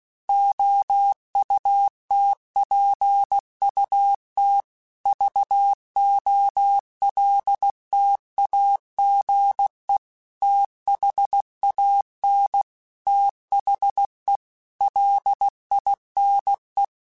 Morse code, heard as a keyed tone, about 16 wpm, 800 Hz. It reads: OUTPUT VOLTAGE THAN THE LINE
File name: Technical Subjects / Technical Subjects-mp3s / Sw Pwr Sup @15